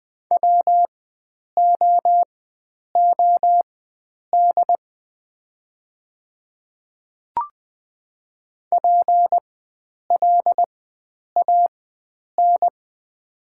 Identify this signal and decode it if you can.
Morse code, heard as a keyed tone, about 20 wpm, spaced as for 9 wpm, 700 Hz. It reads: WOOD PLAN
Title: 4x Character Spacing - Top 500 Words - 20wpm